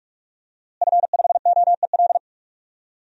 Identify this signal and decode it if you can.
Morse code, heard as a keyed tone, about 46 wpm, 700 Hz. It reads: F5OEL
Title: DXCC-100 Call Signs - 5 Characters - Repeated 2x - 45wpm